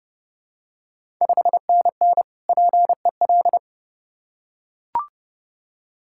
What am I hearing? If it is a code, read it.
Morse code, heard as a keyed tone, about 30 wpm, 700 Hz. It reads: 5NN PEL
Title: Ontario QSO Party Exchange 30wpm